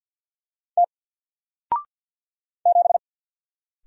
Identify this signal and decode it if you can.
Morse code, heard as a keyed tone, about 50 wpm, 700 Hz. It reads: T 7
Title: Single Letter-Number 50wpm